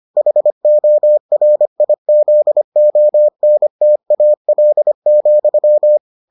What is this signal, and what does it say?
Morse code, heard as a keyed tone, about 25 wpm, 600 Hz. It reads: HORIZONTAL,